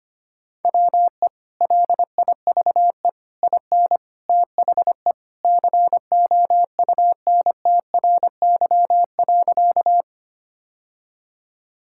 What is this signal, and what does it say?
Morse code, heard as a keyed tone, about 25 wpm, 700 Hz. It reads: WE LIVE IN THE COUNTRY.
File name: practice / 25wpm / Sentences with 3-Word Phrases - 25wpm